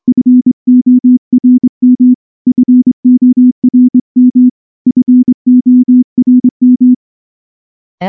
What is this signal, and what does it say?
Morse code, heard as a keyed tone, about 23 wpm, 250 Hz. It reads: FORM FORM FORM
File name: morse / morse-training-25WPM